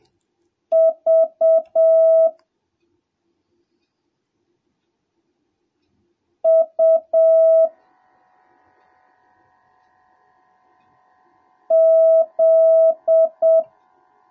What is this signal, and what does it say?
Morse code, heard as a keyed tone, about 7 wpm, 650 Hz. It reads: V U Z